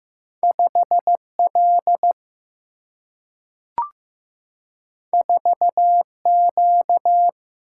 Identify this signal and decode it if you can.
Morse code, heard as a keyed tone, about 15 wpm, 700 Hz. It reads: 5 L 4 Q
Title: Single-Letter Single-Number 15wpm